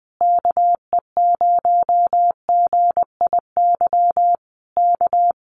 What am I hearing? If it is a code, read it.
Morse code, heard as a keyed tone, about 20 wpm, 700 Hz. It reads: KE0GIY K
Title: Common QSO Phrases Encore 20wpm